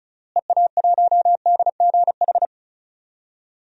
Morse code, heard as a keyed tone, about 36 wpm, 700 Hz. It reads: EA1DGH
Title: DXCC-100 Call Signs - 6 Characters - Repeated 3x - 35wpm